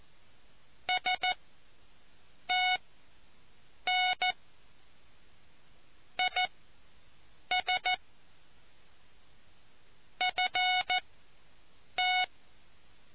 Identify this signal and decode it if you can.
Morse code, heard as a keyed tone, about 14 wpm, spaced as for 6 wpm, 740 Hz. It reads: STN IS FT